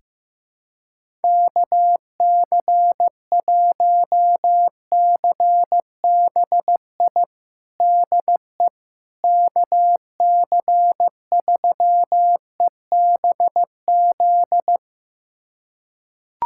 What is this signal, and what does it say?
Morse code, heard as a keyed tone, about 15 wpm, 700 Hz. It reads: KC1CBI DE KC3EBZ
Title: USA Callsign DE Callsign - 6 Characters - 15wpm